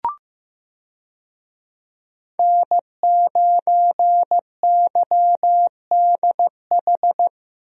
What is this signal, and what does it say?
Morse code, heard as a keyed tone, about 15 wpm, 700 Hz. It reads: N9YDH
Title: USA Call Signs - 5 Characters - Repeated 3x - 15wpm